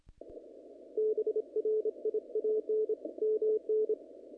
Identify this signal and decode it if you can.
Morse code, heard as a keyed tone, about 26 wpm, 440 Hz. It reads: BRIAN MN
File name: contests / naqp-cw-k5tr-op-k5pi- -1900